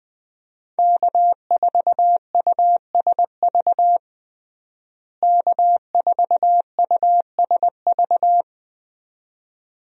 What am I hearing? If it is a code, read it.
Morse code, heard as a keyed tone, about 21 wpm, 700 Hz. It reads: K4USV K4USV
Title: USA Call Signs - 5 Characters - Repeated 2x - 20wpm